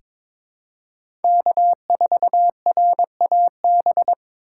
Morse code, heard as a keyed tone, about 22 wpm, 700 Hz. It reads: K4RAB
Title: K1USN SST QSO Practice - 22wpm